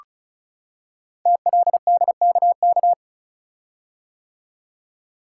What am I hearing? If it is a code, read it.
Morse code, heard as a keyed tone, about 36 wpm, 690 Hz. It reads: TLDKK
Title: Code Groups - 5 Letters - 35wpm